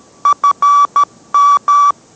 Morse code, heard as a keyed tone, about 13 wpm, 1150 Hz. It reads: FM